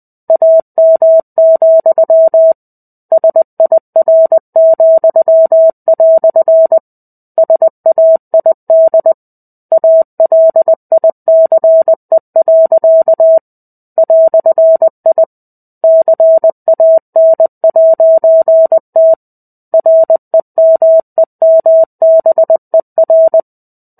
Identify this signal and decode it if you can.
Morse code, heard as a keyed tone, about 20 wpm, 650 Hz. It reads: AM, SIR," SAID ALICE. "I CAN'T REMEMBER
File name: 05-Chapter-05 20 WPM